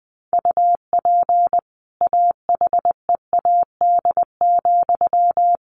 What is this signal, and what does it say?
Morse code, heard as a keyed tone, about 20 wpm, 700 Hz. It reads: UP AHEAD,
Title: A Sound Of Thunder 20wpm